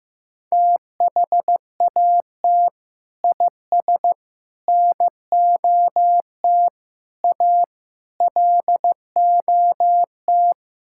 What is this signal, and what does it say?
Morse code, heard as a keyed tone, about 15 wpm, 700 Hz. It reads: THAT IS NOT A LOT
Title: Sentences from Top 1000 Words - Review Part 5 - 15wpm